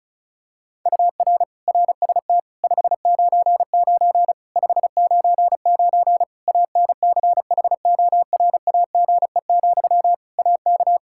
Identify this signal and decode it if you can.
Morse code, heard as a keyed tone, about 36 wpm, 700 Hz. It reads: UR RST 599 599 ANCHORAGE, AK
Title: Practice QSO - Exchange 1 - Beginning - 35wpm